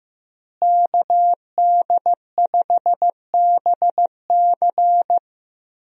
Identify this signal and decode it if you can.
Morse code, heard as a keyed tone, about 15 wpm, 700 Hz. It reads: KD5BC
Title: USA Call Signs - 5 Characters - Repeated 2x - 15wpm